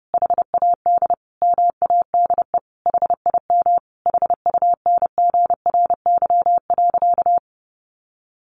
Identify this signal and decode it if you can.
Morse code, heard as a keyed tone, about 28 wpm, 700 Hz. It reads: HAD MADE HIM HUNGRY.
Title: Sentences from Top 1400 Words 30wpm